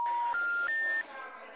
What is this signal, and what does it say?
Special information tone, with noise.
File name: out- -2020- - - .1437